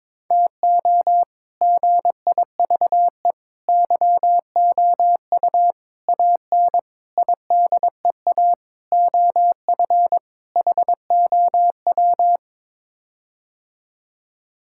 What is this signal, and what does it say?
Morse code, heard as a keyed tone, about 23 wpm, 700 Hz. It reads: TO GIVE YOU AN IDEA OF HOW
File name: practice / 22wpm / Sets of 7 Words - 22wpm